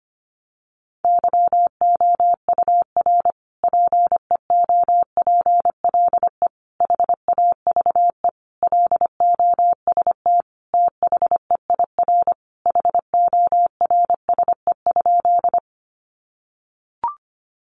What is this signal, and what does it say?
Morse code, heard as a keyed tone, about 25 wpm, 700 Hz. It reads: YOUR PEOPLE HAVE LOST THEIR HORSE?
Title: Sentences from Top 1400 Words 25wpm